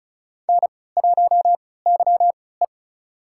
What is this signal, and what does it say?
Morse code, heard as a keyed tone, about 36 wpm, spaced as for 20 wpm, 700 Hz. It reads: N1YE
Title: USA Call Signs - 4 Characters - 3x Character Spacing - 35wpm